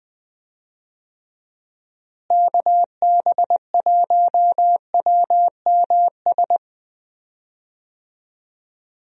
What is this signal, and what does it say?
Morse code, heard as a keyed tone, about 20 wpm, 700 Hz. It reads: KB1WMS
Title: USA Call Signs - 6 Characters - Repeated 2x - 20wpm